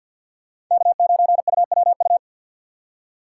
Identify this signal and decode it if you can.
Morse code, heard as a keyed tone, about 53 wpm, 700 Hz. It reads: K9UWU
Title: USA Call Signs - 5 Characters - 2x Character Spacing - 50wpm